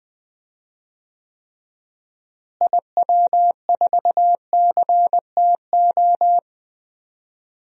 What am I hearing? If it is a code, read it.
Morse code, heard as a keyed tone, about 20 wpm, 700 Hz. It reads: IW4CTO
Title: DXCC-100 Call Signs - 6 Characters - Repeated 3x - 20wpm